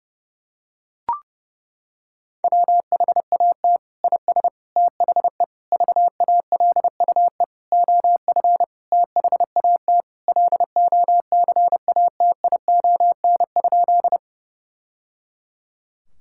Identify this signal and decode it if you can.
Morse code, heard as a keyed tone, about 30 wpm, 700 Hz. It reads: WHAT IS THE VALUE OF THAT LOCATION?
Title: Sentences from Top 1000 Words - Review Part 6 - 30wpm